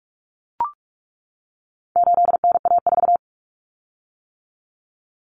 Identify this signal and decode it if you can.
Morse code, heard as a keyed tone, about 44 wpm, 700 Hz. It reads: 8NA4
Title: Common Cut Numbers - 4-Digit 45wpm